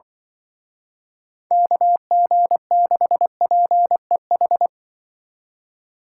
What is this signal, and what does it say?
Morse code, heard as a keyed tone, about 24 wpm, 700 Hz. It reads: KG6PEH
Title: Speed Racing - USA Call Signs 20wpm